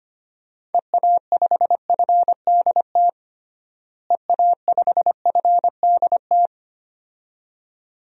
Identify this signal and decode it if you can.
Morse code, heard as a keyed tone, about 25 wpm, 700 Hz. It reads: EA5FDT EA5FDT
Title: DXCC-100 Call Signs - 6 Characters - Repeated 2x - 25wpm